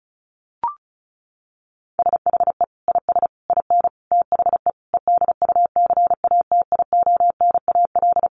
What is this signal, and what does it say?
Morse code, heard as a keyed tone, about 34 wpm, 700 Hz. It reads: SHE IS IN THE EDUCATIONAL
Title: Sentences from Top 1200 Words 35wpm